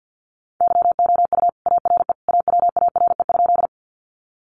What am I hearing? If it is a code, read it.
Morse code, heard as a keyed tone, about 50 wpm, 700 Hz. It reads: YOU ARE AWARE?
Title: Sentences from Top 1200 Words 50wpm